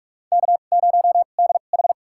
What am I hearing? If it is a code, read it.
Morse code, heard as a keyed tone, about 44 wpm, spaced as for 32 wpm, 700 Hz. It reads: K0DH
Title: USA Call Signs - 2x Character Spacing - 45wpm